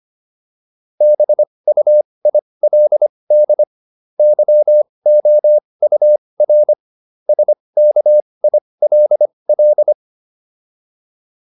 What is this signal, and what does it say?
Morse code, heard as a keyed tone, about 25 wpm, spaced as for 20 wpm, 600 Hz. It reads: BUILD YOUR SKILL